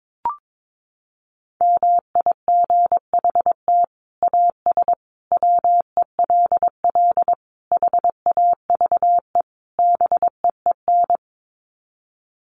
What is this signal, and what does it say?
Morse code, heard as a keyed tone, about 22 wpm, 700 Hz. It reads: MIGHT AS WELL HAVE BEEN
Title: Sets of 5 Words Encore2 22wpm